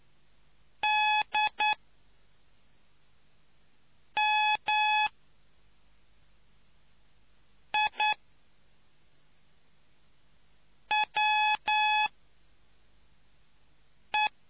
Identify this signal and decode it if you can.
Morse code, heard as a keyed tone, about 9 wpm, spaced as for 3 wpm, 850 Hz. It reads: DMIWE